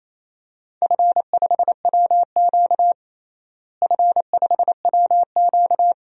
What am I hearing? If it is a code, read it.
Morse code, heard as a keyed tone, about 28 wpm, 700 Hz. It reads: F5WQ F5WQ
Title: DXCC-100 Call Signs - 4 Characters - Repeated 2x - 28wpm